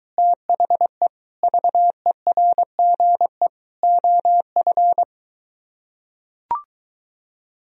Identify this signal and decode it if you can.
Morse code, heard as a keyed tone, about 23 wpm, 700 Hz. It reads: THE VERGE OF